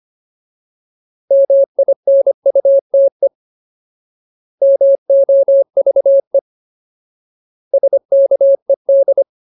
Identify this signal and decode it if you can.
Morse code, heard as a keyed tone, about 25 wpm, 550 Hz. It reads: MINUTE MOVE SKED
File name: Ultimate Word List / 25WPM Ultimate